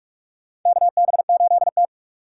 Morse code, heard as a keyed tone, about 46 wpm, 700 Hz. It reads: KB8T
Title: USA Call Signs - 4 Characters - 2x Character Spacing - 45wpm